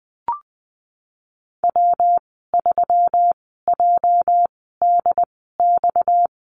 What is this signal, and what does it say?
Morse code, heard as a keyed tone, about 20 wpm, spaced as for 15 wpm, 700 Hz. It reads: W3JDX
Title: 238 Lesson 38g - Letter q - Call Signs 20wpm